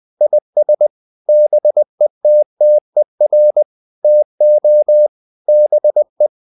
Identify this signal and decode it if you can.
Morse code, heard as a keyed tone, about 20 wpm, 600 Hz. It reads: IS BETTER TO BE
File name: the old man and the sea / the old man and the sea part 1 20wpm